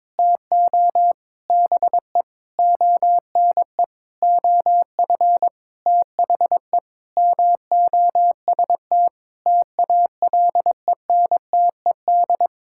Morse code, heard as a keyed tone, about 22 wpm, 700 Hz. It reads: TO BE ONE OF THE MOST TALENTED
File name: practice / 22wpm / Sentences with 8-Word Phrases - 22wpm